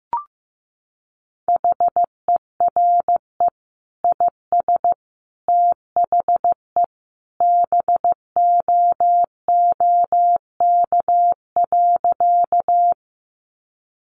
Morse code, heard as a keyed tone, about 15 wpm, 700 Hz. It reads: HERE IS THE BOOK.
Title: Sentences from Top 300 Words 15wpm